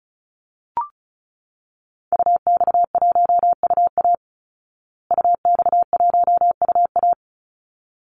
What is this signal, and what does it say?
Morse code, heard as a keyed tone, about 34 wpm, 700 Hz. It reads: UX1UA UX1UA
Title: Call Signs Repeated 3x - 35wpm